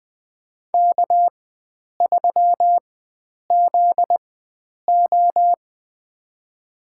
Morse code, heard as a keyed tone, about 21 wpm, spaced as for 9 wpm, 700 Hz. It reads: K3ZO
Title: USA Call Signs - 4 Characters - 4x Character Spacing - 20wpm